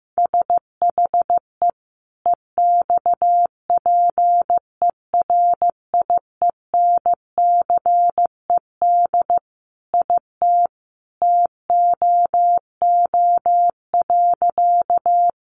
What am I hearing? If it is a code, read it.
Morse code, heard as a keyed tone, about 15 wpm, 700 Hz. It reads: SHE EXPERIENCED IT TOO.
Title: Sentences from Top 1100 Words 15wpm